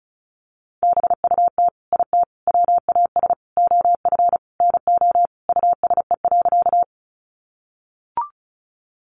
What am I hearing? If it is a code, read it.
Morse code, heard as a keyed tone, about 34 wpm, 700 Hz. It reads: BUT IT WAS OF NO USE.
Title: Sentences from Top 100 Words 35wpm